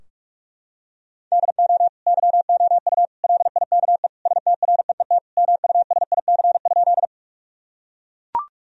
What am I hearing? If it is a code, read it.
Morse code, heard as a keyed tone, about 41 wpm, 700 Hz. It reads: DO YOU LIKE STREET MUSIC?
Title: Sentences from Top 1000 Words - Review Part 3 - 45wpm